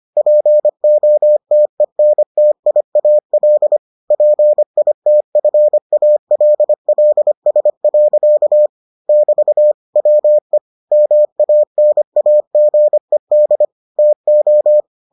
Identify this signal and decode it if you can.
Morse code, heard as a keyed tone, about 25 wpm, 600 Hz. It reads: POTENTIAL PITFALLS. <BT> WE MANAGED TO